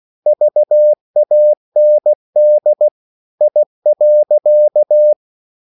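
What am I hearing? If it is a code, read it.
Morse code, heard as a keyed tone, about 16 wpm, 600 Hz. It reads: VAND I.
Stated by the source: Slow (QRS) Morse code at 16 wpm, from CW med Gnister.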